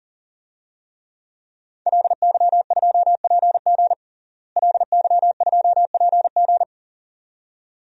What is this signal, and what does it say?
Morse code, heard as a keyed tone, about 40 wpm, 700 Hz. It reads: LY2PG LY2PG
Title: DXCC-100 Call Signs - 5 Characters - Repeated 2x - 40wpm